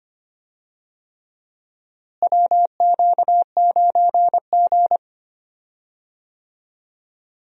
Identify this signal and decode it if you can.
Morse code, heard as a keyed tone, about 25 wpm, 700 Hz. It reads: WQ9G
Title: USA Call Signs - 4 Characters - Repeated 3x - 25wpm